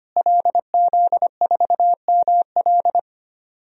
Morse code, heard as a keyed tone, about 25 wpm, 700 Hz. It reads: LZ4ML
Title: DXCC-100 Call Signs - 5 Characters - Repeated 2x - 25wpm